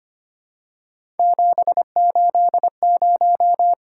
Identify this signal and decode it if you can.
Morse code, heard as a keyed tone, about 25 wpm, 700 Hz. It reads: 780
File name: Three Letter-Number Combinations - 25wpm